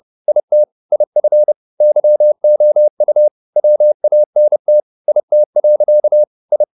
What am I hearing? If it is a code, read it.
Morse code, heard as a keyed tone, about 30 wpm, 600 Hz. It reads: IT IF YOU WANT IT. I